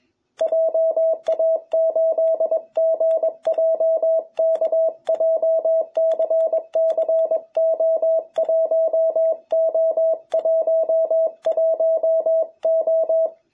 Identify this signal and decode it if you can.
Morse code, heard as a keyed tone, about 21 wpm, 660 Hz. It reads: JA8GJK JCCO1O11O
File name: jh8cbh / 2021cw